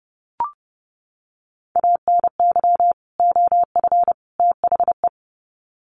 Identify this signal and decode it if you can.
Morse code, heard as a keyed tone, about 30 wpm, 700 Hz. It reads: ANY OF THE
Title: Sets of 3 Words 30wpm